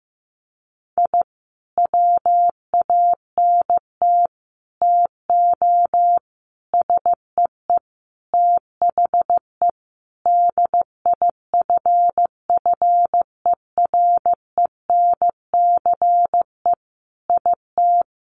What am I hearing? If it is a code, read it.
Morse code, heard as a keyed tone, about 15 wpm, 700 Hz. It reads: I WANT TO SEE THE DIFFERENCE IT
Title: Sentences from Top 1300 Words 15wpm